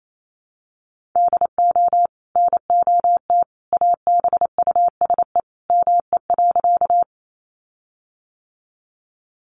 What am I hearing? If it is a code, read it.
Morse code, heard as a keyed tone, about 28 wpm, 700 Hz. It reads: DO NOT ABUSE ME.
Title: Sentences from Top 1300 Words 28wpm